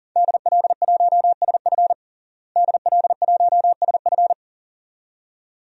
Morse code, heard as a keyed tone, about 40 wpm, 700 Hz. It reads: DL1SF DL1SF
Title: DXCC-100 Call Signs - 5 Characters - Repeated 2x - 40wpm